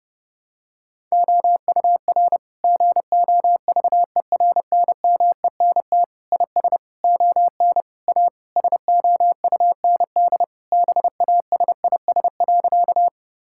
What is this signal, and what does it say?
Morse code, heard as a keyed tone, about 30 wpm, 700 Hz. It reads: OUR GOVERNMENT IS ON A SOUND BASIS.
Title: Sentences from Top 1000 Words - Review Part 5 - 30wpm